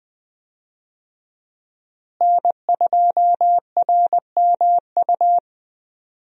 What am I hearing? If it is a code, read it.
Morse code, heard as a keyed tone, about 20 wpm, 700 Hz. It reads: N2RMU
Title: USA Call Signs - 5 Characters - Repeated 2x - 20wpm